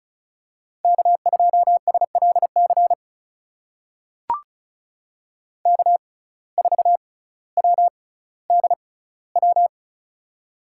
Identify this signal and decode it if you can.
Morse code, heard as a keyed tone, about 34 wpm, spaced as for 18 wpm, 700 Hz. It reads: K2SLC K4WDW
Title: USA Call Signs - 5 Characters - 6x Character Spacing - 35wpm